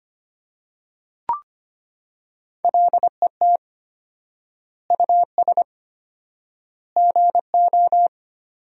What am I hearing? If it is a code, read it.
Morse code, heard as a keyed tone, about 25 wpm, 700 Hz. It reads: LET US GO
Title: 4x Word-Spacing - Sentences from Top 300 Words - 25wpm